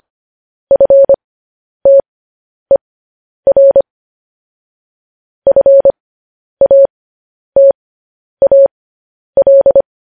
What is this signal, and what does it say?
Morse code, heard as a keyed tone, about 25 wpm, spaced as for 10 wpm, 550 Hz. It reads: FTER FATAL